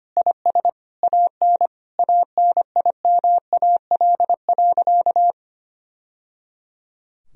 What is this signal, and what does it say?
Morse code, heard as a keyed tone, about 25 wpm, 700 Hz. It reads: IS AN ANIMAL.
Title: Sentences from Top 1000 Words - Review Part 1 - 25wpm